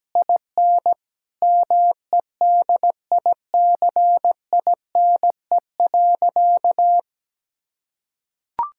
Morse code, heard as a keyed tone, about 17 wpm, 700 Hz. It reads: IN MEDICINE.
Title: Sentences from Top 1000 Words - Review Part 6 - 17wpm